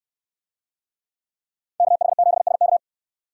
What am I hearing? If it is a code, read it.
Morse code, heard as a keyed tone, about 66 wpm, 700 Hz. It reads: KH7SK